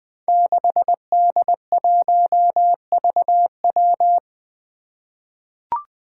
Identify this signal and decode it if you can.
Morse code, heard as a keyed tone, about 20 wpm, 700 Hz. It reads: 6D1VW
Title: Five Challenging Letter-Number Combinations 20wpm